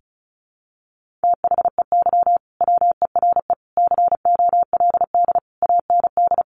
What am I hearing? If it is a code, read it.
Morse code, heard as a keyed tone, about 34 wpm, 700 Hz. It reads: THEY WERE COLD AND
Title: Sentences from Top 1500 Words 35wpm